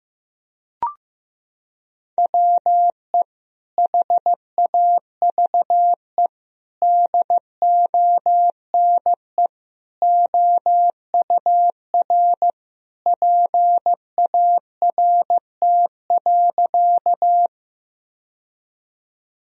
Morse code, heard as a keyed tone, about 15 wpm, 700 Hz. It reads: WE HAVE DONE OUR PART.
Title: Sentences from Top 1000 Words - Review Part 1 - 15wpm